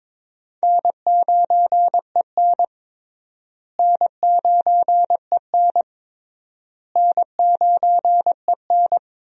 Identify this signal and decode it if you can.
Morse code, heard as a keyed tone, about 22 wpm, 700 Hz. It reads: N9EN N9EN N9EN
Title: USA Call Signs - 4 Characters - Repeated 3x - 22wpm